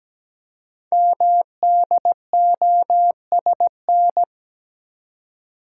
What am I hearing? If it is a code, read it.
Morse code, heard as a keyed tone, about 17 wpm, 700 Hz. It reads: MDOSN
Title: Code Groups - 5 Letter:Numbers - 17wpm